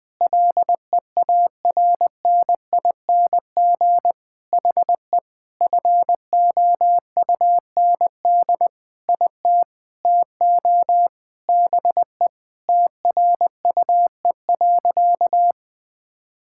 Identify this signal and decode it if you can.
Morse code, heard as a keyed tone, about 20 wpm, 700 Hz. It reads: LEARNING HE FOUND IT TO BE TRUE.
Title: Sentences with 4-Word Phrases - 20wpm